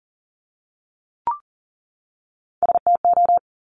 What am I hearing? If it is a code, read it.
Morse code, heard as a keyed tone, about 40 wpm, 700 Hz. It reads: STO
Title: Top Three Letter Combinations in Words 40wpm